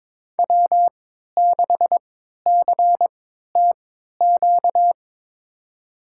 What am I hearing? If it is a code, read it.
Morse code, heard as a keyed tone, about 21 wpm, spaced as for 12 wpm, 700 Hz. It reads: W6CTQ
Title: USA Call Signs - 5 Characters - 3x Character Spacing - 22wpm